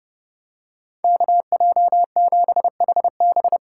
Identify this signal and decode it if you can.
Morse code, heard as a keyed tone, about 28 wpm, 700 Hz. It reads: KJ7HB